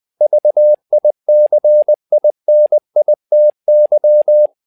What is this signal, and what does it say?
Morse code, heard as a keyed tone, about 20 wpm, 600 Hz. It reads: VICINITY
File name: tower trouble 20wpm